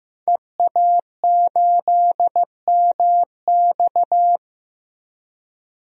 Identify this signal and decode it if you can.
Morse code, heard as a keyed tone, about 15 wpm, 700 Hz. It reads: EA8MX
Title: DXCC-100 Call Signs - 5 Characters - Repeated 3x - 15wpm